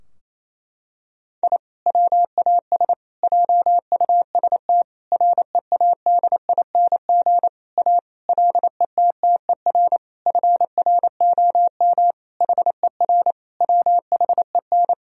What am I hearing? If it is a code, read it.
Morse code, heard as a keyed tone, about 28 wpm, 700 Hz. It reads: I WAS JUST READING A LETTER FROM HER WHEN
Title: Sentences from Top 1000 Words - Review Part 3 - 28wpm